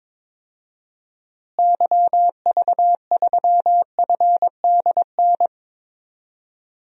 Morse code, heard as a keyed tone, about 22 wpm, 700 Hz. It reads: YV3FDN
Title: DXCC-100 Call Signs - 6 Characters - Repeated 3x - 22wpm